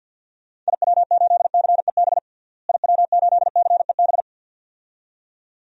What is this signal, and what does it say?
Morse code, heard as a keyed tone, about 47 wpm, 700 Hz. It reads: IW8CEB IW8CEB
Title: DXCC-100 Call Signs - 6 Characters - Repeated 2x - 50wpm